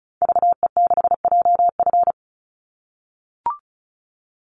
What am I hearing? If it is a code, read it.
Morse code, heard as a keyed tone, about 34 wpm, 700 Hz. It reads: VE6JF
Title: Call Signs - 35wpm